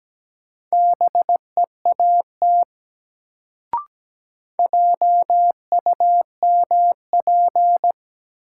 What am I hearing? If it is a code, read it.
Morse code, heard as a keyed tone, about 17 wpm, 700 Hz. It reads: BEAT JUMP
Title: Sentences from Top 1300 Words - New Word Drill - 17wpm